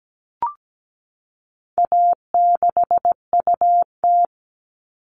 Morse code, heard as a keyed tone, about 17 wpm, 700 Hz. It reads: A6UT
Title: All Cut Numbers - 4-Digit 17wpm